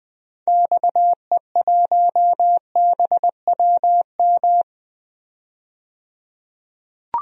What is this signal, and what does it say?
Morse code, heard as a keyed tone, about 20 wpm, 700 Hz. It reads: XE1BWM
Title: DXCC-100 Call Signs - 6 Characters - 20wpm